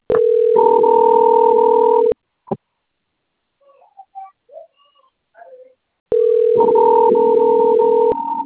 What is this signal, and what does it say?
Ringback tone (North America), 440 + 480 Hz.